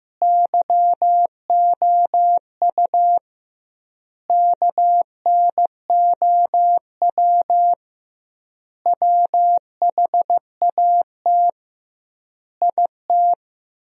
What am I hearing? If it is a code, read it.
Morse code, heard as a keyed tone, about 15 wpm, 700 Hz. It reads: YOU KNOW WHAT IT
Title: 2x Word-Spacing - Sentences from Top 100 Words - 15wpm